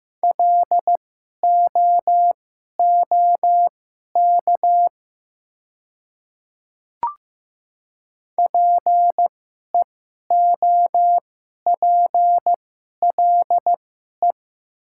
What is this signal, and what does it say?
Morse code, heard as a keyed tone, about 15 wpm, spaced as for 11 wpm, 700 Hz. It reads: LOOK PEOPLE
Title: 2x Character Spacing - Top 100 Words - 15wpm